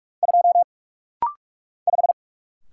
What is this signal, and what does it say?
Morse code, heard as a keyed tone, about 46 wpm, 700 Hz. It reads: 2 5
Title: Single Letter-Number - Rapid-Fire 45wpm